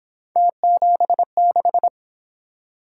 Morse code, heard as a keyed tone, about 26 wpm, 700 Hz. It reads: T76
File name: Speed-Racing - Three Letter-Number Combinations - 22wpm